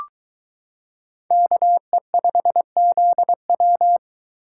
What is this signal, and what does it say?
Morse code, heard as a keyed tone, about 22 wpm, 700 Hz. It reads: KE5ZW